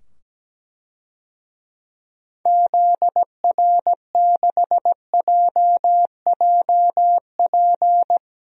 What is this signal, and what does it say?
Morse code, heard as a keyed tone, about 17 wpm, 700 Hz. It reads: ZR6JJP
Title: DXCC-100 Call Signs - 6 Characters - 17wpm